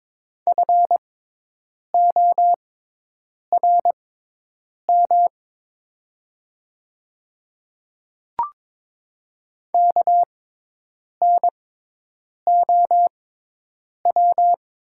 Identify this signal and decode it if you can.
Morse code, heard as a keyed tone, about 22 wpm, spaced as for 8 wpm, 700 Hz. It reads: FORM KNOW
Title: 6x Character Spacing - Top 200 Words - 22wpm